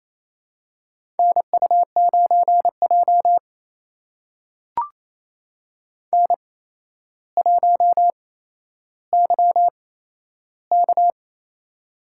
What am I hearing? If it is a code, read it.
Morse code, heard as a keyed tone, about 28 wpm, spaced as for 12 wpm, 700 Hz. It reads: NU9J N1YK
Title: USA Call Signs - 4 Characters - 8x Character Spacing - 28wpm